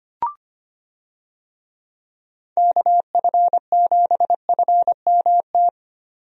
Morse code, heard as a keyed tone, about 25 wpm, 700 Hz. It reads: KF7FMT
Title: USA Call Signs - 6 Characters - 25wpm